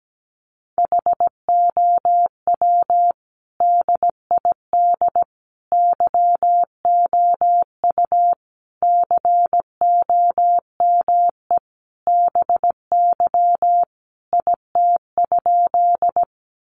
Morse code, heard as a keyed tone, about 17 wpm, 700 Hz. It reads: HOW DID YOU COME BY IT?
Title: Sentences from Top 100 Words 17wpm